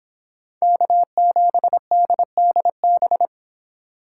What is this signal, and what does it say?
Morse code, heard as a keyed tone, about 26 wpm, 700 Hz. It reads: K7DDB